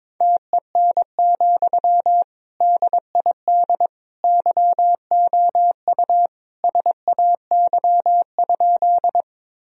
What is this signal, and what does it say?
Morse code, heard as a keyed tone, about 22 wpm, 700 Hz. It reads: TEN, DID YOU SAY?
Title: Sentences from Top 1000 Words - Review Part 2 - 22wpm